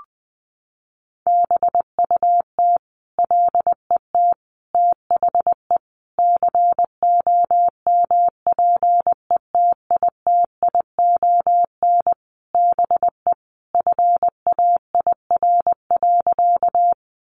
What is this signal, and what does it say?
Morse code, heard as a keyed tone, about 20 wpm, 700 Hz. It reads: BUT LET THE COMPETITION BE FAIR.
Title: Sentences from Top 1300 Words 20wpm